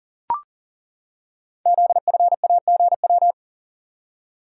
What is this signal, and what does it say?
Morse code, heard as a keyed tone, about 41 wpm, 690 Hz. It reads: ZFAGW
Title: Code Groups - 5 Letters - 40wpm